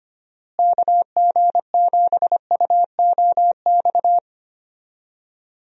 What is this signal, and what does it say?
Morse code, heard as a keyed tone, about 25 wpm, 700 Hz. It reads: KG7UOX
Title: USA Call Signs - 6 Characters - Repeated 2x - 25wpm